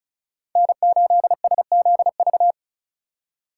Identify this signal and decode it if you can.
Morse code, heard as a keyed tone, about 34 wpm, 700 Hz. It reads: N8SZV
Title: USA Call Signs - 5 Characters - 2x Character Spacing - 35wpm